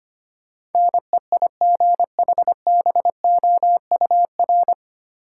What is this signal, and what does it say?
Morse code, heard as a keyed tone, about 25 wpm, 700 Hz. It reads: NEIGHBOUR
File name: practice / 25wpm / 9 Letter Words - First 3 Letters Given - 25wpm